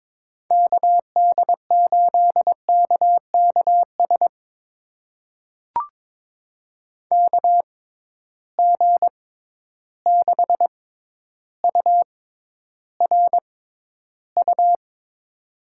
Morse code, heard as a keyed tone, about 22 wpm, spaced as for 11 wpm, 700 Hz. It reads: KD8KKS KG6URU
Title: USA Call Signs - 6 Characters - 6x Character Spacing - 22wpm